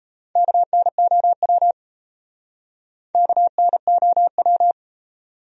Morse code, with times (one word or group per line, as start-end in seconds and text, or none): 0.4-1.7 KNOW
3.1-4.7 KNOW